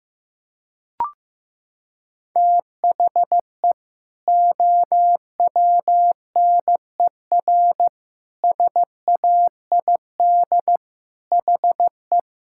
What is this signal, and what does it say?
Morse code, heard as a keyed tone, about 15 wpm, 700 Hz. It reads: THE OWNER SAID HE
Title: Sentences from Top 1600 Words 15wpm